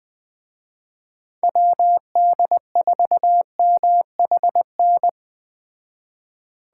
Morse code, heard as a keyed tone, about 20 wpm, 700 Hz. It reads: WD4MHN
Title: Speed Racing - USA 2x3 Call Signs 17wpm